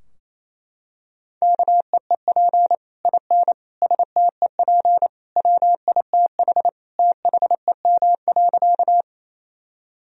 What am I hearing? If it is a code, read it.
Morse code, heard as a keyed tone, about 28 wpm, 700 Hz. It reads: KEEP IN STEP WITH THEM.
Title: Sentences from Top 1000 Words - Review Part 2 - 28wpm